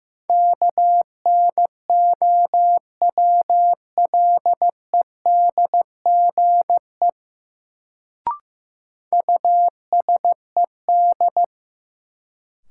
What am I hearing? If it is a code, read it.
Morse code, heard as a keyed tone, about 15 wpm, 700 Hz. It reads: KNOWLEDGE USED
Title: Sentences from Top 600 Words - New Word Drill - 15wpm